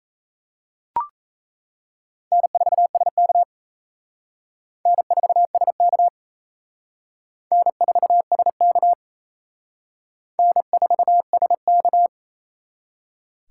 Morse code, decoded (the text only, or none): N4SK N4SK N4SK N4SK